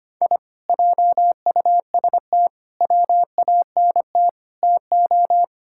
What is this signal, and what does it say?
Morse code, heard as a keyed tone, about 25 wpm, 700 Hz. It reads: I JUST WANT TO
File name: practice / 25wpm / Sentences with 4-Word Phrases - 25wpm